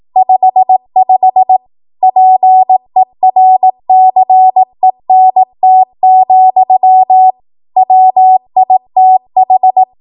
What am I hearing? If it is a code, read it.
Morse code, heard as a keyed tone, about 18 wpm, 750 Hz. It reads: 55 PERCENT, WITH
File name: Archive / 18 WPM / 18WPM